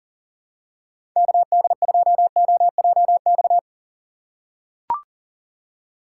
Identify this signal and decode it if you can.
Morse code, heard as a keyed tone, about 40 wpm, 700 Hz. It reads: KD2OJX
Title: USA Call Signs - 4x Character Spacing - 40wpm